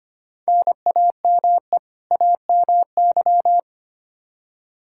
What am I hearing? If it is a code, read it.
Morse code, heard as a keyed tone, about 25 wpm, 700 Hz. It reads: NAME AMY
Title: Practice QSO - Exchange 1 - Beginning - 25wpm